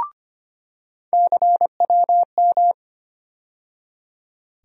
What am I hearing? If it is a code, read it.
Morse code, heard as a keyed tone, about 25 wpm, 700 Hz. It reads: CWM
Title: USA Call Sign Postfixes - 3 Letters 25wpm